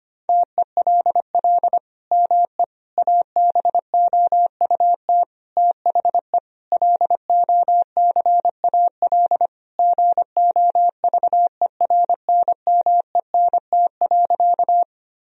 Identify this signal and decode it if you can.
Morse code, heard as a keyed tone, about 25 wpm, 700 Hz. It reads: TELL ME ABOUT THE LOCAL GOVERNMENT.
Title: Sentences from Top 1000 Words - Review Part 3 - 25wpm